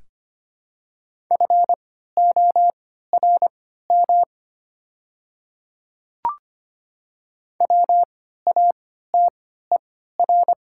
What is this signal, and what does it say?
Morse code, heard as a keyed tone, about 25 wpm, spaced as for 14 wpm, 700 Hz. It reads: FORM WATER
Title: 3x Character Spacing - Top 500 Words - 25wpm